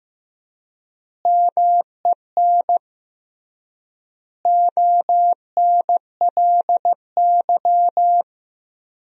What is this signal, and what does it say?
Morse code, heard as a keyed tone, about 15 wpm, 700 Hz. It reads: MEN ONLY
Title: 3x Word-Spacing - Sentences from Top 500 Words - 15wpm